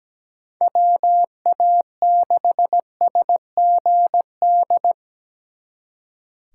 Morse code, heard as a keyed tone, about 20 wpm, 700 Hz. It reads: WA6SGD
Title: Speed Racing - USA Call Signs 17wpm